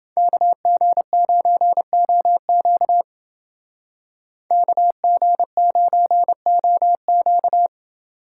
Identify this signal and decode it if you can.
Morse code, heard as a keyed tone, about 28 wpm, 700 Hz. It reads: KG9OQ KG9OQ